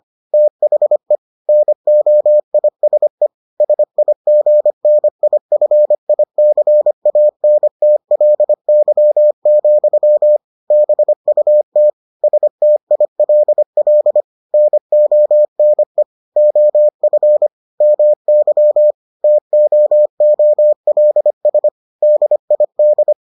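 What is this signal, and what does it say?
Morse code, heard as a keyed tone, about 25 wpm, 600 Hz. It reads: THE NOISE SIGNIFICANTLY, BUT STILL NONE OF MY TOOLS DID
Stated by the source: Morse edition of the Foundations of Amateur Radio episodes.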